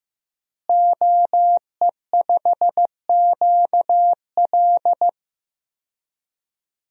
Morse code, heard as a keyed tone, about 15 wpm, 700 Hz. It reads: OE5QL
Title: DXCC-100 Call Signs - 5 Characters - Repeated 2x - 15wpm